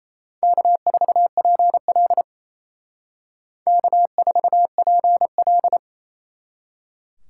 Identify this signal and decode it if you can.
Morse code, heard as a keyed tone, about 35 wpm, 700 Hz. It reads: K4PL K4PL
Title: Speed Racing - USA Call Signs 28wpm